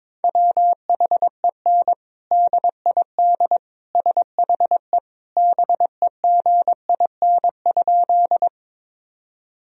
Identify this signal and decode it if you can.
Morse code, heard as a keyed tone, about 22 wpm, 700 Hz. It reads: WHEN DID SHE BEGIN?
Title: Sentences from Top 1000 Words - Review Part 1 - 22wpm